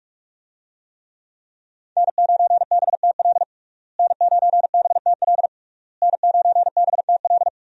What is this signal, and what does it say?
Morse code, heard as a keyed tone, about 41 wpm, 700 Hz. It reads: N9BTL N9BTL N9BTL
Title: USA Call Signs - 5 Characters - Repeated 3x - 45wpm